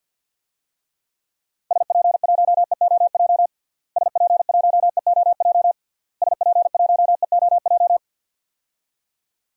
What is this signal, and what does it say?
Morse code, heard as a keyed tone, about 52 wpm, 700 Hz. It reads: SP1EOJ SP1EOJ SP1EOJ
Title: DXCC-100 Call Signs - 6 Characters - Repeated 3x - 50wpm